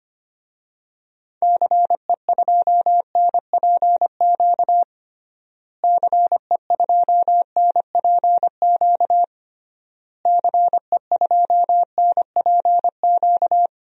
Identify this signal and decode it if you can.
Morse code, heard as a keyed tone, about 25 wpm, 700 Hz. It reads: CE2NPQ CE2NPQ CE2NPQ
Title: DXCC-100 Call Signs - 6 Characters - Repeated 3x - 25wpm